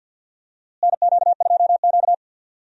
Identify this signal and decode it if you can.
Morse code, heard as a keyed tone, about 51 wpm, 700 Hz. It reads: NQ2Q
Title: USA Call Signs - 4 Characters - Repeated 3x - 50wpm